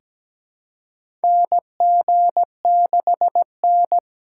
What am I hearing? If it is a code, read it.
Morse code, heard as a keyed tone, about 17 wpm, 700 Hz. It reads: NG6N
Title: USA Call Signs - 4 Characters - 3x Character Spacing - 17wpm